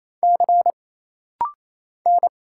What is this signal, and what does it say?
Morse code, heard as a keyed tone, about 28 wpm, 700 Hz. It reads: C N
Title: Single Letters - Rapid-Fire 28wpm